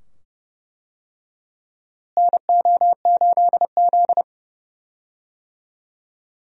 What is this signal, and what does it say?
Morse code, heard as a keyed tone, about 31 wpm, 700 Hz. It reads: NO8Z
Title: USA Call Signs - 4 Characters - Repeated 2x - 30wpm